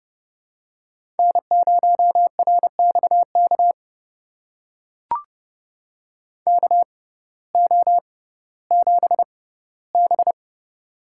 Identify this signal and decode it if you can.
Morse code, heard as a keyed tone, about 30 wpm, spaced as for 17 wpm, 700 Hz. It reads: N0RXK KO7B
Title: USA Call Signs - 6x Character Spacing - 30wpm